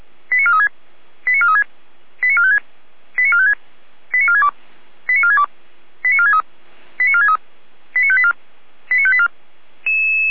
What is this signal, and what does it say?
Five-tone selective call (ZVEI), heard as quick runs of five tones, tones of about 70 ms. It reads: 89436 89436 89456 89456 89462 89462 89463 89463 89584 89584